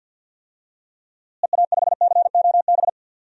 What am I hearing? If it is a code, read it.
Morse code, heard as a keyed tone, about 51 wpm, 700 Hz. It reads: EA5COB
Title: DXCC-100 Call Signs - 6 Characters - Repeated 3x - 50wpm